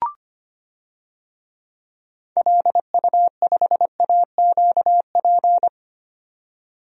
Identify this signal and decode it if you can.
Morse code, heard as a keyed tone, about 25 wpm, 700 Hz. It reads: LU5AQP
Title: DXCC-100 Call Signs - 6 Characters - 25wpm